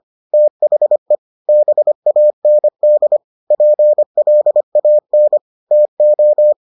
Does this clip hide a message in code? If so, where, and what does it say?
Yes: Morse code from 0.3 to 6.5 s, heard as a keyed tone, about 25 wpm, 600 Hz. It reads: THE BAND PLAN TO